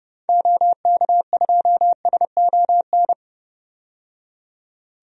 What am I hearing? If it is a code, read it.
Morse code, heard as a keyed tone, about 31 wpm, 700 Hz. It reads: OK2SON
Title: DXCC-100 Call Signs - 6 Characters - 30wpm